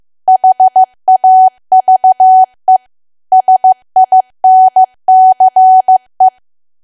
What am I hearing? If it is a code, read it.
Morse code, heard as a keyed tone, about 15 wpm, 750 Hz. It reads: HAVE SINCE
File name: cwa / W1AW-Level2 / 15WPM